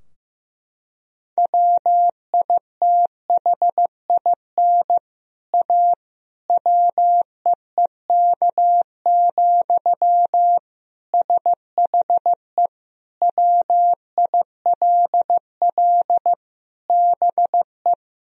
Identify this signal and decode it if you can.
Morse code, heard as a keyed tone, about 15 wpm, 700 Hz. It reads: WITHIN A WEEK, SHE WILL BE
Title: Sentences from Top 1000 Words - Review Part 4 - 15wpm